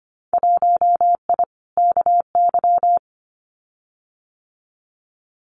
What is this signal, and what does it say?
Morse code, heard as a keyed tone, about 25 wpm, 700 Hz. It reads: 1I KY
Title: Winter Field Day Exchanges 25wpm